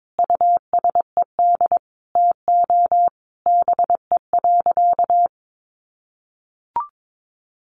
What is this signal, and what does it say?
Morse code, heard as a keyed tone, about 22 wpm, 700 Hz. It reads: USED TO BE.
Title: Sentences from Top 1100 Words 22wpm